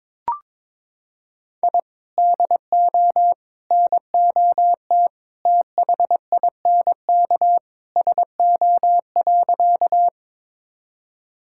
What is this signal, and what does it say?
Morse code, heard as a keyed tone, about 21 wpm, 700 Hz. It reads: I DO NOT THINK SO.
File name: practice / 22wpm / Sentences with 3-Word Phrases - 22wpm